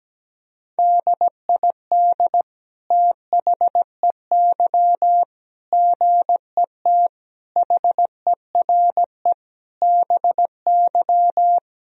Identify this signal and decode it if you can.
Morse code, heard as a keyed tone, about 17 wpm, 700 Hz. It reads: DID THEY GET HERE BY
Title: Sentences from Top 1000 Words - Review Part 3 - 17wpm